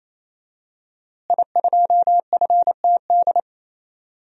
Morse code, heard as a keyed tone, about 28 wpm, 700 Hz. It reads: I2FTD
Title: DXCC-100 Call Signs - 5 Characters - Repeated 3x - 28wpm